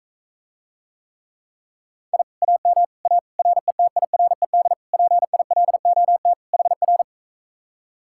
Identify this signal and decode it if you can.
Morse code, heard as a keyed tone, about 44 wpm, 700 Hz. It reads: I AM A RETIRED PILOT HR